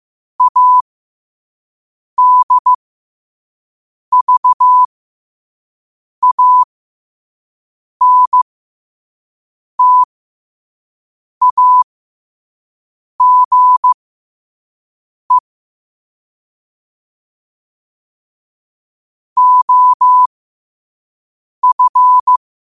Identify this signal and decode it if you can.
Morse code, heard as a keyed tone, about 15 wpm, spaced as for 5 wpm, 1000 Hz. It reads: ADVANTAGE OF
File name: qotdfiles / 5wpm